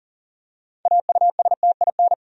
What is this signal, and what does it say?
Morse code, heard as a keyed tone, about 40 wpm, 700 Hz. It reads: AUSTIN
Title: USA Common Names - 40wpm